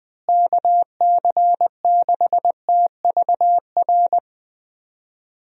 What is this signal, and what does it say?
Morse code, heard as a keyed tone, about 20 wpm, 700 Hz. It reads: KC6TVR
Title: USA Call Signs - 6 Characters - Repeated 3x - 20wpm